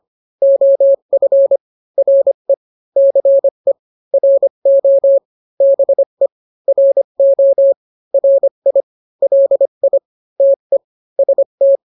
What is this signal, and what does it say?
Morse code, heard as a keyed tone, about 25 wpm, 550 Hz. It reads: OF RE CE RO BE RO RI LI TE ST